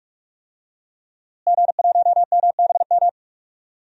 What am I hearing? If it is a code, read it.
Morse code, heard as a keyed tone, about 45 wpm, 700 Hz. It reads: G1MBM
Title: DXCC-100 Call Signs - 5 Characters - Repeated 2x - 45wpm